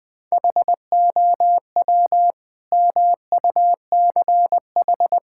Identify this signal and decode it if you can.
Morse code, heard as a keyed tone, about 20 wpm, 700 Hz. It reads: HOW MUCH
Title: Sentences from Top 1000 Words - Review Part 5 - 20wpm